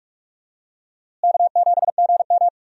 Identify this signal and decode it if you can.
Morse code, heard as a keyed tone, about 47 wpm, 700 Hz. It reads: K7GM
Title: USA Call Signs - 4 Characters - Repeated 2x - 45wpm